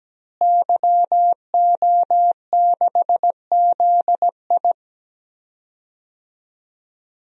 Morse code, heard as a keyed tone, about 17 wpm, 700 Hz. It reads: YO6ZI
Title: DXCC-100 Call Signs - 5 Characters - Repeated 2x - 17wpm